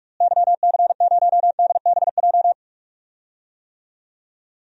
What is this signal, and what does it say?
Morse code, heard as a keyed tone, about 47 wpm, 700 Hz. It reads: YC0DBJ
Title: DXCC-100 Call Signs - 6 Characters - Repeated 2x - 45wpm